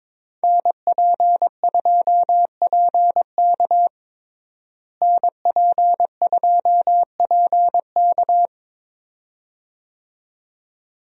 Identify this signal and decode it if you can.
Morse code, heard as a keyed tone, about 22 wpm, 700 Hz. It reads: NP2PK NP2PK
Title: USA Call Signs - 5 Characters - Repeated 2x - 22wpm